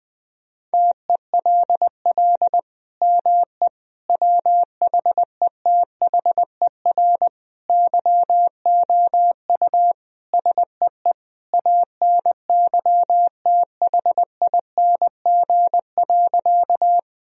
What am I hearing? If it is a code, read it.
Morse code, heard as a keyed tone, about 20 wpm, 700 Hz. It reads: TELL ME WHETHER YOU SEE ANYTHING.
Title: Sentences from Top 1000 Words - Review Part 3 - 20wpm